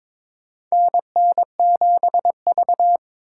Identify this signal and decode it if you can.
Morse code, heard as a keyed tone, about 22 wpm, 700 Hz. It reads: NN7V
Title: USA Call Signs - 4 Characters - 4x Character Spacing - 22wpm